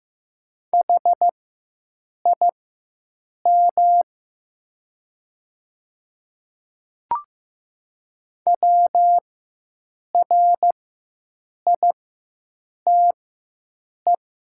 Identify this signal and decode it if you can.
Morse code, heard as a keyed tone, about 15 wpm, spaced as for 7 wpm, 700 Hz. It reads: HIM WRITE
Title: 4x Character Spacing - Top 100 Words - 15wpm